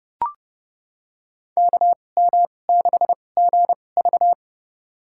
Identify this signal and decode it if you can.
Morse code, heard as a keyed tone, about 32 wpm, spaced as for 22 wpm, 700 Hz. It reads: KM6GV
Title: USA Call Signs - 5 Characters - 2x Character Spacing - 30wpm